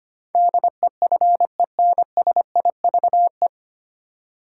Morse code, heard as a keyed tone, about 25 wpm, 700 Hz. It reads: DEFENSIVE
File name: practice / 25wpm / 9 Letter Words - First 3 Letters Given - 25wpm